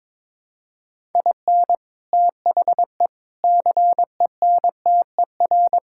Morse code, heard as a keyed tone, about 22 wpm, 700 Hz. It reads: IN THE CENTER
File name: practice / 22wpm / Build A Phrase - 5-Word Phrases - 22wpm